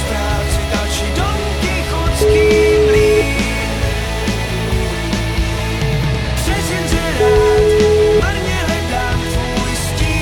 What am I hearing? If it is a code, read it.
Ringback tone (Europe, 425 Hz), with noise.